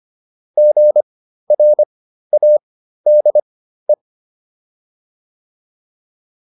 Morse code, heard as a keyed tone, about 25 wpm, spaced as for 13 wpm, 600 Hz. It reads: GRADE